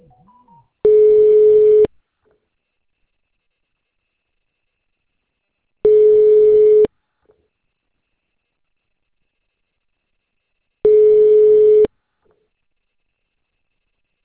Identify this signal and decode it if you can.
Ringback tone (Europe, 425 Hz).